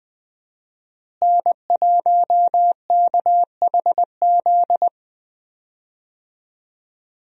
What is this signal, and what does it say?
Morse code, heard as a keyed tone, about 20 wpm, 700 Hz. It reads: N1KHZ
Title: USA Call Signs - 5 Characters - Repeated 2x - 20wpm